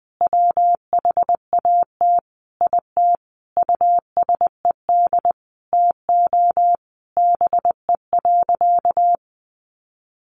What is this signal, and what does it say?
Morse code, heard as a keyed tone, about 20 wpm, 700 Hz. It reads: WHAT IT USED TO BE.
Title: Sentences from Top 1100 Words 20wpm